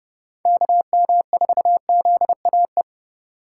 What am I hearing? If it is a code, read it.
Morse code, heard as a keyed tone, about 30 wpm, 700 Hz. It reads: KM4ZAE
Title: USA Call Signs - 6 Characters - 4x Character Spacing - 30wpm